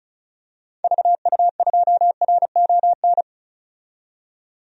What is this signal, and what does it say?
Morse code, heard as a keyed tone, about 36 wpm, 700 Hz. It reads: VU2RON
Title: DXCC-100 Call Signs - 6 Characters - 35wpm